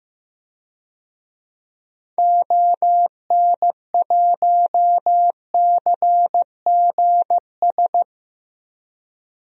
Morse code, heard as a keyed tone, about 15 wpm, 700 Hz. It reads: ON1CGS
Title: DXCC-100 Call Signs - 6 Characters - 15wpm